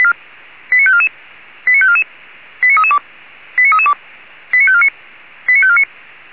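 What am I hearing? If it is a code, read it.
Five-tone selective call (ZVEI), heard as quick runs of five tones, tones of about 70 ms. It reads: 89544 89544 89332 89332 89549 89549
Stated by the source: ZVEI five-tone selective call received by SDR at Feuerwehr Orlamünde.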